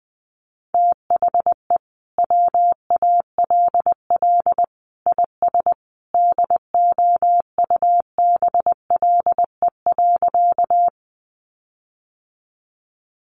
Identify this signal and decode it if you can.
Morse code, heard as a keyed tone, about 20 wpm, 700 Hz. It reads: THE WALL IS DOUBLE.
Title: Sentences from Top 1400 Words 20wpm